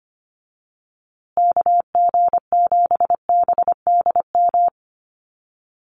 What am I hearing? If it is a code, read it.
Morse code, heard as a keyed tone, about 25 wpm, 700 Hz. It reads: KG7BDM
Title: USA Call Signs Repeated 2x 25wpm